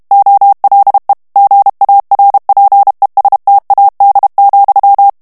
Morse code, heard as a keyed tone, about 32 wpm, 800 Hz. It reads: OLE GARPESTAD,
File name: morse mp3 / ESPAGNOL TEXTE EN CLAIR 1 VITESSE 30 MOTS MINUTES